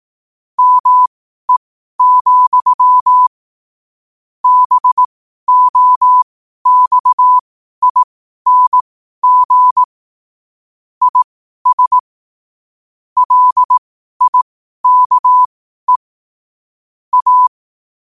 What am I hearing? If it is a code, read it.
Morse code, heard as a keyed tone, about 18 wpm, spaced as for 12 wpm, 1000 Hz. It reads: ME, BOXING IS LIKE A
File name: qotdfiles / 12wpm